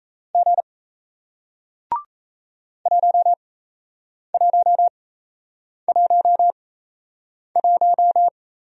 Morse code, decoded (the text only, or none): G 1111